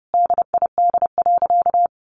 Morse code, heard as a keyed tone, about 30 wpm, 700 Hz. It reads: DID.
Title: Sentences from Top 100 Words 30wpm